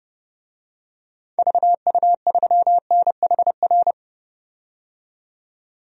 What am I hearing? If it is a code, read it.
Morse code, heard as a keyed tone, about 32 wpm, 700 Hz. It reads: VU3NHR
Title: DXCC-100 Call Signs - 6 Characters - 30wpm